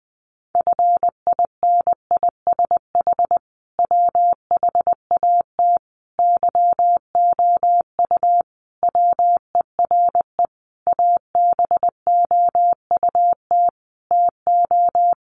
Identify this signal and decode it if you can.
Morse code, heard as a keyed tone, about 20 wpm, 700 Hz. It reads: FINISH WHAT YOU WERE ABOUT TO
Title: Sentences from Top 1200 Words 20wpm